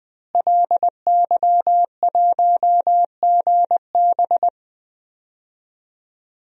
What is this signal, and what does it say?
Morse code, heard as a keyed tone, about 21 wpm, 700 Hz. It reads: LY1GB
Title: DXCC-100 Call Signs - 5 Characters - Repeated 3x - 20wpm